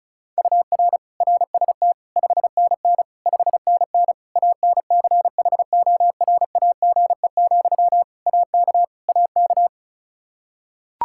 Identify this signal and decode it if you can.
Morse code, heard as a keyed tone, about 36 wpm, 700 Hz. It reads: UR RST 5NN 5NN ANCHORAGE, AK AK
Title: Practice QSO - Exchange 1 - Beginning - 35wpm